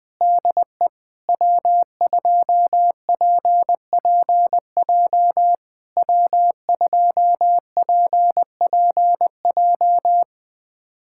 Morse code, heard as a keyed tone, about 21 wpm, 700 Hz. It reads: DE W2PPJ W2PPJ
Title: SOTA QSO Practice 20wpm